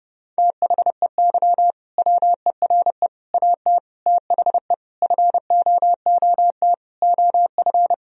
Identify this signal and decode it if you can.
Morse code, heard as a keyed tone, about 30 wpm, 700 Hz. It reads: THEY WERE AT THE FOOT OF
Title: Sentences from Top 1000 Words - Review Part 3 - 30wpm